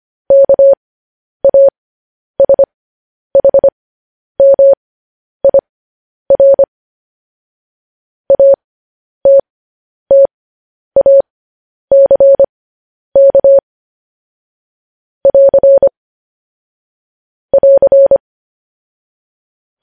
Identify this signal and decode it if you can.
Morse code, heard as a keyed tone, about 25 wpm, spaced as for 10 wpm, 550 Hz. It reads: KASHMIR ATTACK <AR> <AR>
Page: news headlines in morse code at 10 wpm